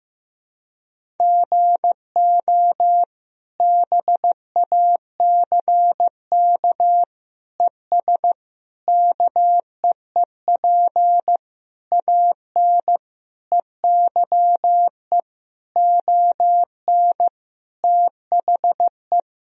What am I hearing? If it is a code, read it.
Morse code, heard as a keyed tone, about 15 wpm, 700 Hz. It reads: GO BACK ES KEEP AN EYE ON THE
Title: Sentences from Top 300 Words and Abbreviations 15wpm